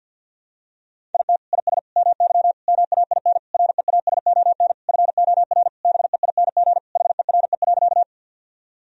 Morse code, heard as a keyed tone, about 48 wpm, 700 Hz. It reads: IT IS MY MAIN REASON FOR BEING HERE.
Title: Sentences from Top 1000 Words - Review Part 2 - 50wpm